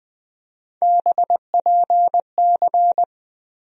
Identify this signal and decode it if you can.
Morse code, heard as a keyed tone, about 21 wpm, 700 Hz. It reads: BPC
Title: USA Call Sign Postfixes - 3 Letters 20wpm